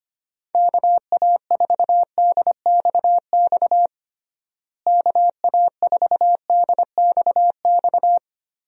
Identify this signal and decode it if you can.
Morse code, heard as a keyed tone, about 25 wpm, 700 Hz. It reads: KA4DXX KA4DXX
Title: USA Call Signs - 6 Characters - Repeated 3x - 25wpm